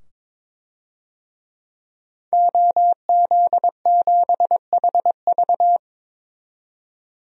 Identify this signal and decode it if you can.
Morse code, heard as a keyed tone, about 21 wpm, 700 Hz. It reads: OZ7HV
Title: DXCC-100 Call Signs - 5 Characters - 22wpm